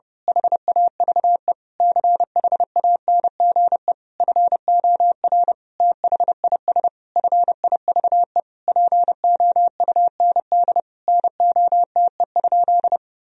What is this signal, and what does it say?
Morse code, heard as a keyed tone, about 30 wpm, 700 Hz. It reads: HAVE CHANGE FOR THIS FIVE POUND NOTE?
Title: Sentences from Top 1000 Words - Review Part 2 - 30wpm